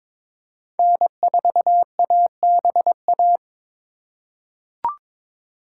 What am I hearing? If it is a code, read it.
Morse code, heard as a keyed tone, about 22 wpm, 700 Hz. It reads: N4ABA
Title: USA Call Signs - 5 Characters - 2x Character Spacing - 22wpm